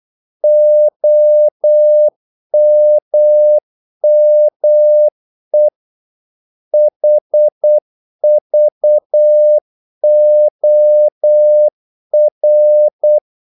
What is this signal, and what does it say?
Morse code, heard as a keyed tone, about 8 wpm, 600 Hz. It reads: OMME HVOR
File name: Korte ord DK 8wpm